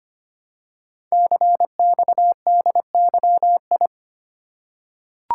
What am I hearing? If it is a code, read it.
Morse code, heard as a keyed tone, about 25 wpm, 700 Hz. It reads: CXDYI
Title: Code Groups - 5 Letters - 25wpm